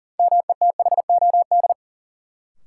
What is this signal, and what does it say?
Morse code, heard as a keyed tone, about 40 wpm, 700 Hz. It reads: METHOD
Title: Sentences from Top 600 Words - New Word Drill - 40wpm